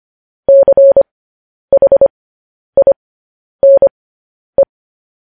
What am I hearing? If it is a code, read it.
Morse code, heard as a keyed tone, about 25 wpm, spaced as for 10 wpm, 550 Hz. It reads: CHINE
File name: podcast / News headlines-1x-25x10